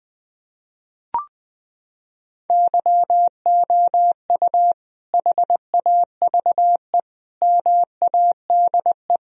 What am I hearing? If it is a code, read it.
Morse code, heard as a keyed tone, about 20 wpm, 700 Hz. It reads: YOU HAVE MADE
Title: Sentences from Top 1000 Words - Review Part 1 - 20wpm